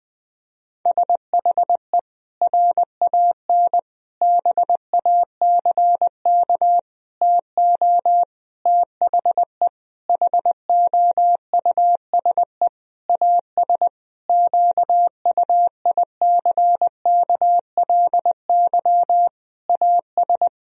Morse code, heard as a keyed tone, about 20 wpm, 700 Hz. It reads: SHE RAN BACK TO THE HOUSE AS QUICKLY AS
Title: Sentences from Top 1000 Words - Review Part 3 - 20wpm